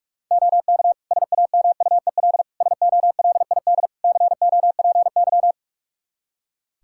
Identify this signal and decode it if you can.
Morse code, heard as a keyed tone, about 43 wpm, 700 Hz. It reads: OK SAMUEL SOLID COPY
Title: K1USN SST QSO Practice 45wpm